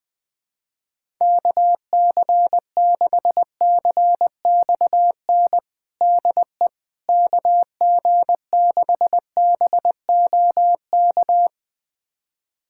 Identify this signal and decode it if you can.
Morse code, heard as a keyed tone, about 20 wpm, 700 Hz. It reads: KC6CXN DE KG6BOK
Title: USA Callsign DE Callsign - 6 Characters - 20wpm